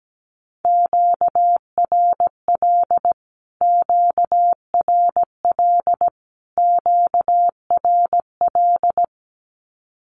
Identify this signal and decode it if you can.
Morse code, heard as a keyed tone, about 17 wpm, 700 Hz. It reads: QRL QRL QRL
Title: Q-Codes Repeated 3x 17wpm